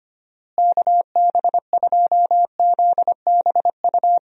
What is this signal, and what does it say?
Morse code, heard as a keyed tone, about 25 wpm, 700 Hz. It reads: KB2ZBU
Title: USA Call Signs - 6 Characters - Repeated 3x - 25wpm